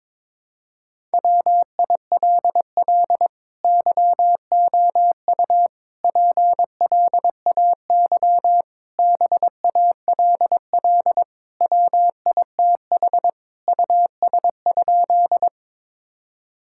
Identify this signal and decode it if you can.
Morse code, heard as a keyed tone, about 22 wpm, 700 Hz. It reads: WILL YOU PLAY BALL WITH US?
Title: Sentences from Top 1000 Words - Review Part 6 - 22wpm